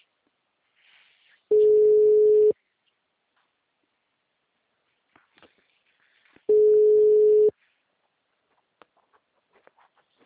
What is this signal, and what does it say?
Ringback tone (Europe, 425 Hz).